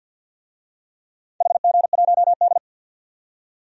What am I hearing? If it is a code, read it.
Morse code, heard as a keyed tone, about 50 wpm, 700 Hz. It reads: HG1D
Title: DXCC-100 Call Signs - 4 Characters - 50wpm